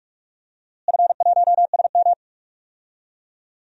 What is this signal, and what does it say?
Morse code, heard as a keyed tone, about 45 wpm, 700 Hz. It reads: F1SM
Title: DXCC-100 Call Signs - 4 Characters - Repeated 3x - 45wpm